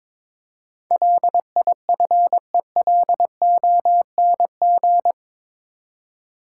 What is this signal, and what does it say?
Morse code, heard as a keyed tone, about 21 wpm, 700 Hz. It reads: LIFELONG
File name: practice / 22wpm / Eight Letter Words - 22wpm